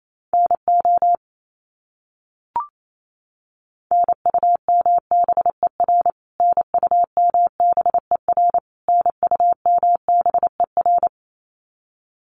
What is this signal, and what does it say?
Morse code, heard as a keyed tone, about 28 wpm, 700 Hz. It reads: NO NUMBER NUMBER NUMBER
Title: Top 300 Words Repeated 28wpm